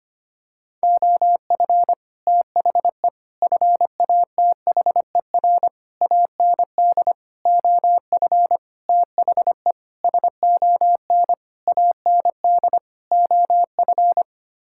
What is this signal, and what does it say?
Morse code, heard as a keyed tone, about 25 wpm, 700 Hz. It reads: OF THE FATHER AND OF THE SON AND OF
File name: Sets of 9 Words - 25wpm